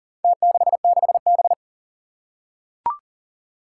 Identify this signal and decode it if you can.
Morse code, heard as a keyed tone, about 40 wpm, 700 Hz. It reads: T66B
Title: Four Challenging Letter-Number Combinations 40wpm